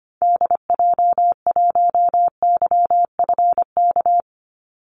Morse code, heard as a keyed tone, about 25 wpm, 700 Hz. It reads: DJ1YFK
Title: Call Signs Repeated 2x - 25wpm